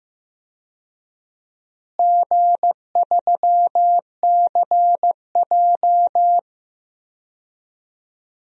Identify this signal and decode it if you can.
Morse code, heard as a keyed tone, about 15 wpm, 700 Hz. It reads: G3CJ
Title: DXCC-100 Call Signs - 4 Characters - Repeated 3x - 15wpm